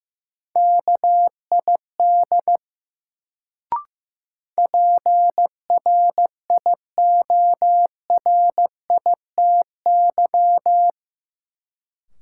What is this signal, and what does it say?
Morse code, heard as a keyed tone, about 15 wpm, 700 Hz. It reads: KID PRIORITY
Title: Sentences from Top 1600 Words - New Word Drill - 15wpm